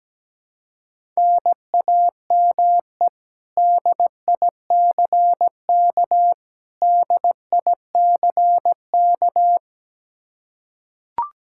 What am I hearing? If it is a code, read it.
Morse code, heard as a keyed tone, about 17 wpm, 700 Hz. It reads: NAME DICK DICK
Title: Practice QSO - Exchange 1 - Beginning - 17wpm